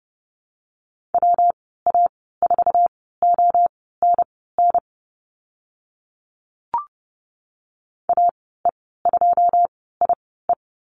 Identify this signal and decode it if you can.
Morse code, heard as a keyed tone, about 30 wpm, spaced as for 17 wpm, 700 Hz. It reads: WA4ONN AE2IE
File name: Lesson 19g - Letter w - Call Signs 30wpm